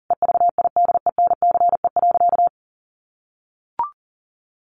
Morse code, heard as a keyed tone, about 37 wpm, 700 Hz. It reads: EVIDENCE.
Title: Sentences from Top 1300 Words 40wpm